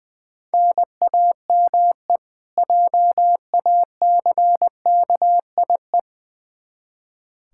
Morse code, heard as a keyed tone, about 21 wpm, 700 Hz. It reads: NAME JACKIE
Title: K1USN SST QSO Practice 20wpm